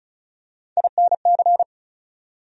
Morse code, heard as a keyed tone, about 36 wpm, 700 Hz. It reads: INC
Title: USA Call Sign Postfixes - 3 Letters 35wpm